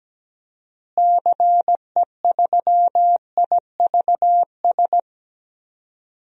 Morse code, heard as a keyed tone, about 17 wpm, 700 Hz. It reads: CE3IVS
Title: DXCC-100 Call Signs - 6 Characters - Repeated 2x - 17wpm